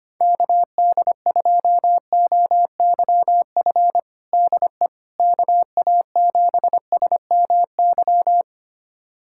Morse code, heard as a keyed tone, about 25 wpm, 700 Hz. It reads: KD2OYF DE KA7SMY
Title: USA Callsign DE Callsign - 6 Characters - 25wpm